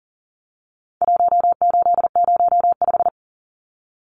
Morse code, heard as a keyed tone, about 40 wpm, 700 Hz. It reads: 1805
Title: Numbers - Contest Style - 4-Digit 40wpm